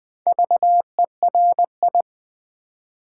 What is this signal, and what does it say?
Morse code, heard as a keyed tone, about 20 wpm, 700 Hz. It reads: VERI
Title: Build A Word - 2 Letters At A Time - 6-Letter Words - 20wpm